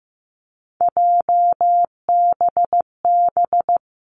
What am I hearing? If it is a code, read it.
Morse code, heard as a keyed tone, about 15 wpm, 700 Hz. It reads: JBB
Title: Three Challenging Letter-Number Combinations 15wpm